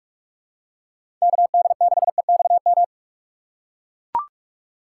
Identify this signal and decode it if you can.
Morse code, heard as a keyed tone, about 42 wpm, 700 Hz. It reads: KD6EXM
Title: USA Call Signs - 4x Character Spacing - 45wpm